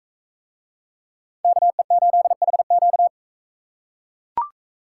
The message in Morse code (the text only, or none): KE8HQ